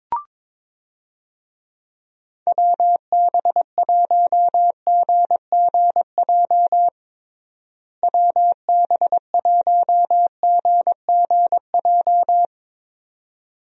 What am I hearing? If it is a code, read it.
Morse code, heard as a keyed tone, about 22 wpm, 700 Hz. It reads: WB1GGJ WB1GGJ
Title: USA Call Signs - 6 Characters - Repeated 2x - 22wpm